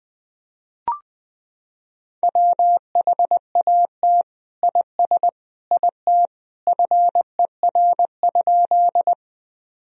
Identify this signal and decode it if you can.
Morse code, heard as a keyed tone, about 20 wpm, 700 Hz. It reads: WHAT IS IT FER?
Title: Sentences from Top 100 Words and Abbreviations 20wpm